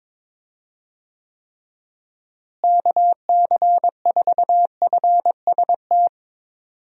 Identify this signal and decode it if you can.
Morse code, heard as a keyed tone, about 22 wpm, 700 Hz. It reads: KC4FST
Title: USA Call Signs - 6 Characters - Repeated 2x - 22wpm